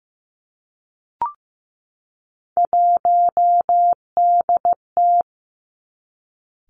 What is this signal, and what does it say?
Morse code, heard as a keyed tone, about 15 wpm, 700 Hz. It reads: 1DT
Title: Three Challenging Letter-Number Combinations 15wpm